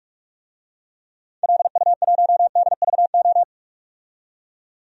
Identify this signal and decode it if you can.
Morse code, heard as a keyed tone, about 46 wpm, 700 Hz. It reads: LU1DVO
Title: DXCC-100 Call Signs - 6 Characters - 45wpm